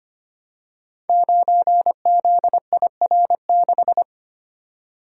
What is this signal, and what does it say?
Morse code, heard as a keyed tone, about 25 wpm, 700 Hz. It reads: 9ZIR6
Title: Code Groups - 5 Letter:Numbers - 25wpm